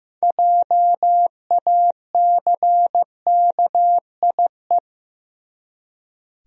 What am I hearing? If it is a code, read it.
Morse code, heard as a keyed tone, about 16 wpm, 700 Hz. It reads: JACKIE
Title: USA Common Names - 15wpm